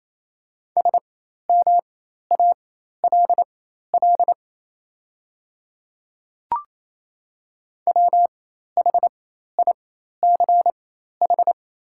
Morse code, heard as a keyed tone, about 27 wpm, spaced as for 13 wpm, 690 Hz. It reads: SMALL WHICH
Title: 4x Character Spacing - Top 300 Words - 28wpm